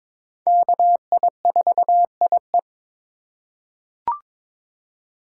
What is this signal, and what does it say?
Morse code, heard as a keyed tone, about 22 wpm, 700 Hz. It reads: KI4IE
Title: USA Call Signs - 5 Characters - 3x Character Spacing - 22wpm